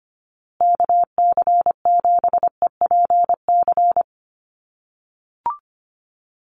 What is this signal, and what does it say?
Morse code, heard as a keyed tone, about 25 wpm, 700 Hz. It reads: KC7EPC
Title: USA Call Signs Repeated 3x 25wpm